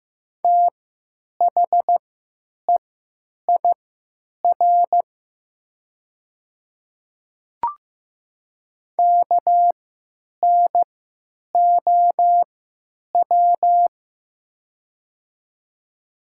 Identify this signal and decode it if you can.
Morse code, heard as a keyed tone, about 15 wpm, spaced as for 9 wpm, 700 Hz. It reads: THEIR KNOW
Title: 3x Character Spacing - Top 100 Words - 15wpm